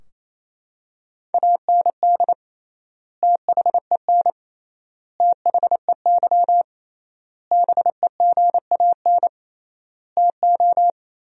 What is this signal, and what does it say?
Morse code, heard as a keyed tone, about 28 wpm, 700 Hz. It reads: AND THEN THEY BEGAN TO
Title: 3x Word-Spacing - Sentences from Top 300 Words - 28wpm